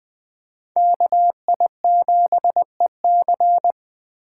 Morse code, heard as a keyed tone, about 20 wpm, 700 Hz. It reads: KI7EC
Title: USA Call Signs - 5 Characters - 8x Character Spacing - 20wpm